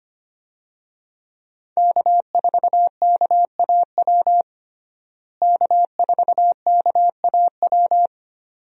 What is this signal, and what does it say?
Morse code, heard as a keyed tone, about 25 wpm, 700 Hz. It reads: K4KAW K4KAW
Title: USA Call Signs - 5 Characters - Repeated 2x - 25wpm